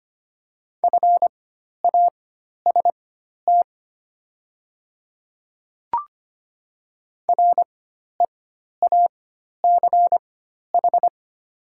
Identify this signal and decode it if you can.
Morse code, heard as a keyed tone, about 25 wpm, spaced as for 12 wpm, 700 Hz. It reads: FAST REACH
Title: 4x Character Spacing - Top 500 Words - 25wpm